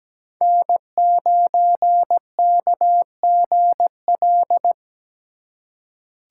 Morse code, heard as a keyed tone, about 17 wpm, 700 Hz. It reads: N9KGL
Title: USA Call Signs - 5 Characters - 3x Character Spacing - 17wpm